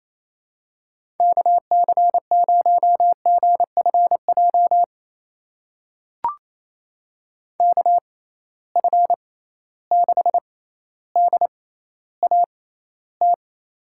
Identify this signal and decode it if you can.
Morse code, heard as a keyed tone, about 28 wpm, spaced as for 14 wpm, 700 Hz. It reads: KC0GFJ KF6DAT
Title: USA Call Signs - 6 Characters - 6x Character Spacing - 28wpm